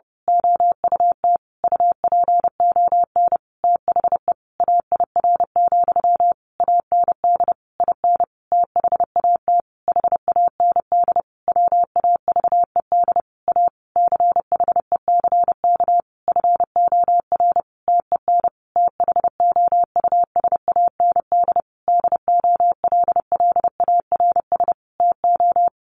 Morse code, heard as a keyed tone, about 30 wpm, 700 Hz. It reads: OUT UPON THE AIR, AND IN THAT HAND WAVED A CHECK FOR TEN THOUSAND DOLLARS TO
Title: A Sound Of Thunder 30wpm